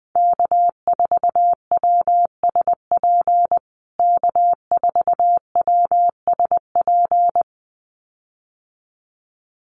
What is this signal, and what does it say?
Morse code, heard as a keyed tone, about 19 wpm, 700 Hz. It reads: K4WSP K4WSP
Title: USA Call Signs Repeated 2x 20wpm